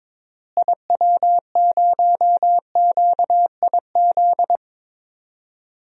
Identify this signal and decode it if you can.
Morse code, heard as a keyed tone, about 22 wpm, 700 Hz. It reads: IW0QIZ
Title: DXCC-100 Call Signs - 6 Characters - Repeated 3x - 22wpm